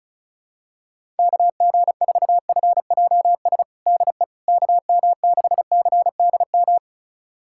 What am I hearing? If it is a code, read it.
Morse code, heard as a keyed tone, about 34 wpm, 700 Hz. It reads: KG4FJS DE KM6CDM
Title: USA Callsign DE Callsign 35wpm